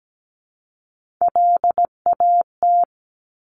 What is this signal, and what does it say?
Morse code, heard as a keyed tone, about 17 wpm, 700 Hz. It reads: LAT
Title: Top Three Letter Combinations in Words 17wpm